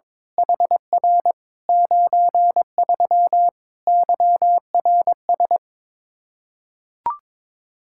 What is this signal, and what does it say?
Morse code, heard as a keyed tone, about 22 wpm, 700 Hz. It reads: HR 93 YRS
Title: Practice QSO - Exchange 3 - More Info - 22wpm